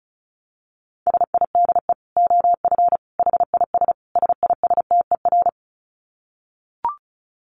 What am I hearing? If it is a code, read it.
Morse code, heard as a keyed tone, about 34 wpm, 700 Hz. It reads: SIDE OF HIS SISTER
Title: Dit Challenge 35wpm